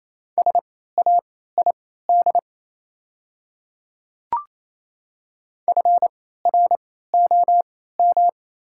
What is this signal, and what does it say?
Morse code, heard as a keyed tone, about 28 wpm, spaced as for 16 wpm, 690 Hz. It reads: SAID FROM
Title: 3x Character Spacing - Top 100 Words - 28wpm